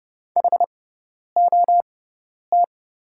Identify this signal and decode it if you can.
Morse code, heard as a keyed tone, about 30 wpm, spaced as for 10 wpm, 700 Hz. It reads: HOT
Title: 6x Character Spacing - Top 200 Words - 30wpm